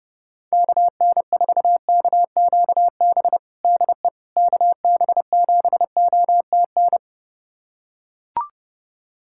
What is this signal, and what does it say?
Morse code, heard as a keyed tone, about 30 wpm, 700 Hz. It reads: KN4KQB DE KB7OTN
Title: USA Callsign DE Callsign 30wpm